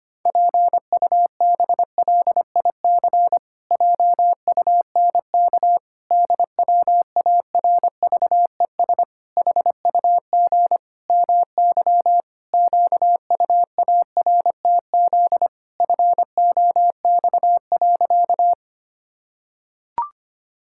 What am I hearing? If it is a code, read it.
Morse code, heard as a keyed tone, about 25 wpm, 700 Hz. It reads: PUBLIC JUNK DWARVES HUG MY QUARTZ FOX.
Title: Pangram Challenge v2 - 25wpm